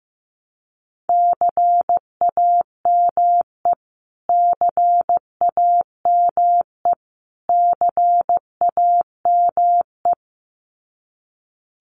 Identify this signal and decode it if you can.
Morse code, heard as a keyed tone, about 15 wpm, 700 Hz. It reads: CAME CAME CAME
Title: Top 500 Words Repeated 15wpm